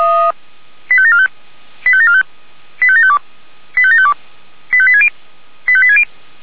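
Five-tone selective call (ZVEI), heard as quick runs of five tones, tones of about 70 ms. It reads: 85745 85745 85752 85752 85860 85860
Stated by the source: ZVEI five-tone selective call 85726 received on an SDR.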